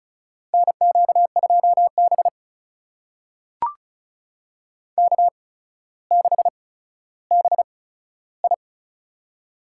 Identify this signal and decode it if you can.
Morse code, heard as a keyed tone, about 36 wpm, spaced as for 15 wpm, 700 Hz. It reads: NQ2B K6BI
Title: USA Call Signs - 4 Characters - 8x Character Spacing - 35wpm